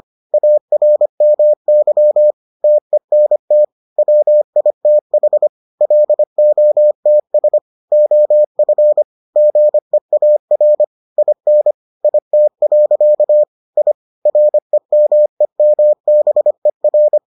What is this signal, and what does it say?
Morse code, heard as a keyed tone, about 25 wpm, 600 Hz. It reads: ARMY TENT WITH LOTS OF GEAR IN IT. I REMEMBER